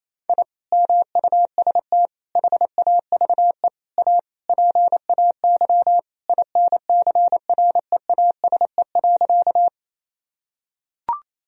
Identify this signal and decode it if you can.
Morse code, heard as a keyed tone, about 28 wpm, 700 Hz. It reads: I MUST HAVE A PAY INCREASE.
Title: Sentences from Top 1000 Words - Review Part 4 - 28wpm